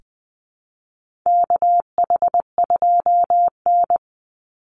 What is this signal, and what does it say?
Morse code, heard as a keyed tone, about 19 wpm, 700 Hz. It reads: KH2N
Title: Call Signs - 20wpm